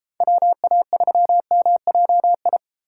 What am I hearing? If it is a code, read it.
Morse code, heard as a keyed tone, about 37 wpm, 700 Hz. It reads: WA3MJI